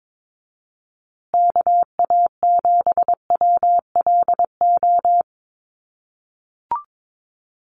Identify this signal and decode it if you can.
Morse code, heard as a keyed tone, about 22 wpm, 700 Hz. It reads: KA7WLO
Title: USA Call Signs Repeated 2x 22wpm